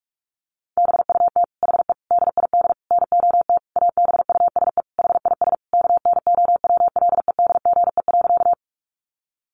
Morse code, heard as a keyed tone, about 44 wpm, 700 Hz. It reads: BUT HE DID NOT ABUSE HIS KNOWLEDGE.
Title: Sentences from Top 1300 Words 45wpm